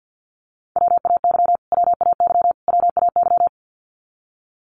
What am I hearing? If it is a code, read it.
Morse code, heard as a keyed tone, about 50 wpm, 700 Hz. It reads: WAY WAY WAY
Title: Top 500 Words Repeated 50wpm